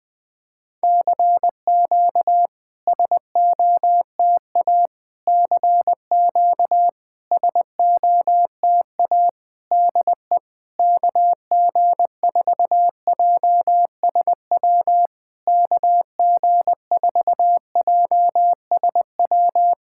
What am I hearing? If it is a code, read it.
Morse code, heard as a keyed tone, about 20 wpm, 700 Hz. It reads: CQ SOTA CQ SOTA DE KG4JSW KG4JSW
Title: SOTA QSO Practice 20wpm